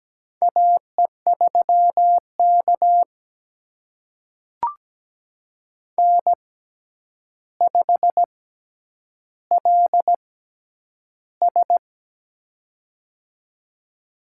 Morse code, heard as a keyed tone, about 17 wpm, spaced as for 9 wpm, 700 Hz. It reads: AE3K N5LS
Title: USA Call Signs - 4 Characters - 6x Character Spacing - 17wpm